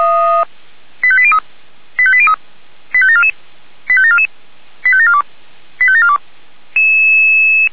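Five-tone selective call (ZVEI), heard as quick runs of five tones, tones of about 70 ms. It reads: 85803 85803 85744 85744 85742 85742
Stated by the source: ZVEI five-tone selective call 85753 received by SDR.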